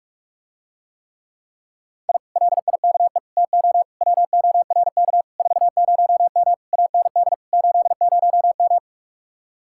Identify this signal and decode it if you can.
Morse code, heard as a keyed tone, about 48 wpm, 700 Hz. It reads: I LIKE TO WORK 40M AND 80M